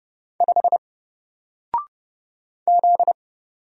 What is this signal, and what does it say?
Morse code, heard as a keyed tone, about 30 wpm, 700 Hz. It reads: 5 Z
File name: Single-Letter-Number-30wpm